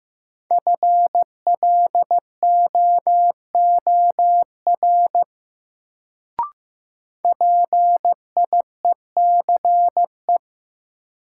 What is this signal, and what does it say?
Morse code, heard as a keyed tone, about 15 wpm, 700 Hz. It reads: FLOOR PIECE
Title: Sets of 3 Words - New Word Drill - 15wpm